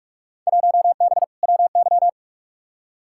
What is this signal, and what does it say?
Morse code, heard as a keyed tone, about 42 wpm, 700 Hz. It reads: 1B WY
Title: Field Day Exchanges Encore - 45wpm